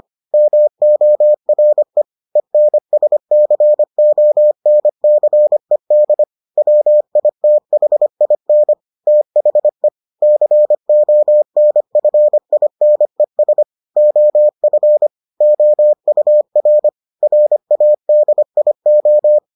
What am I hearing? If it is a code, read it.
Morse code, heard as a keyed tone, about 25 wpm, 600 Hz. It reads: MORE ENSCONCED WITHIN THE CONFINES OF OUR RADIO